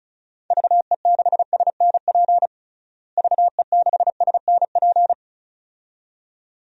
Morse code, heard as a keyed tone, about 34 wpm, 700 Hz. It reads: VE6SNP VE6SNP
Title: DXCC-100 Call Signs - 6 Characters - Repeated 2x - 35wpm